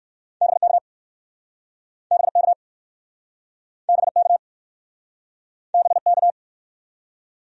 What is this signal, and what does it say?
Morse code, heard as a keyed tone, about 60 wpm, 700 Hz. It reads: BK BK BK BK